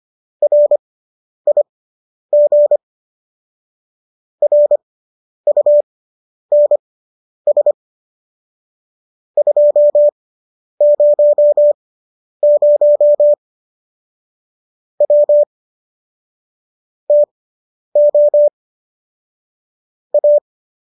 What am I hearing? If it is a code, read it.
Morse code, heard as a keyed tone, about 25 wpm, spaced as for 10 wpm, 590 Hz. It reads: RIG RUNS 200 W TO A